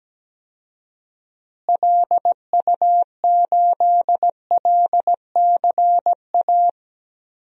Morse code, heard as a keyed tone, about 17 wpm, 700 Hz. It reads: LU8LCA
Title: DXCC-100 Call Signs - 6 Characters - Repeated 2x - 17wpm